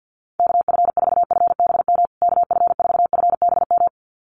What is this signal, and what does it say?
Morse code, heard as a keyed tone, about 48 wpm, 700 Hz. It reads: KF4FBM KF4FBM
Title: USA Call Signs Repeated 2x 50wpm